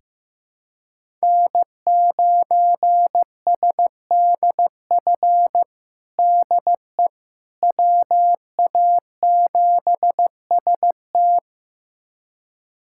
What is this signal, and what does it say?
Morse code, heard as a keyed tone, about 15 wpm, 700 Hz. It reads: N9SDF DE WA7ST
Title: USA Callsign DE Callsign - 5 Characters - 15wpm